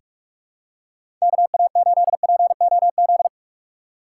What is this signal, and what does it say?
Morse code, heard as a keyed tone, about 47 wpm, 700 Hz. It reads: KA8POZ
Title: K1USN SST QSO Practice - 45wpm